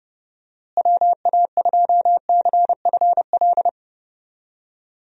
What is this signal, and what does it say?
Morse code, heard as a keyed tone, about 30 wpm, 700 Hz. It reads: WA2CFL
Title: USA Call Signs - 6 Characters - Repeated 3x - 30wpm